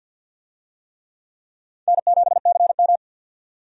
Morse code, heard as a keyed tone, about 53 wpm, 700 Hz. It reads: N7CM
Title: USA Call Signs - 4 Characters - 50wpm